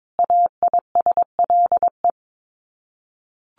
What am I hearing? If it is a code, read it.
Morse code, heard as a keyed tone, about 22 wpm, 700 Hz. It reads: AISLE
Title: Sets of 5 Words Encore2 22wpm